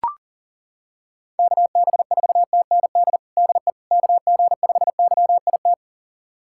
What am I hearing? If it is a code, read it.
Morse code, heard as a keyed tone, about 37 wpm, 690 Hz. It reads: KB4TND DE KG5YIT
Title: USA Callsign DE Callsign - 6 Characters - 40wpm